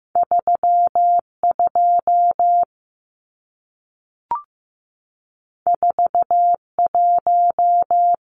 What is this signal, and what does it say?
Morse code, heard as a keyed tone, about 15 wpm, 700 Hz. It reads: 32 41
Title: Numbers - 2-Digit 15wpm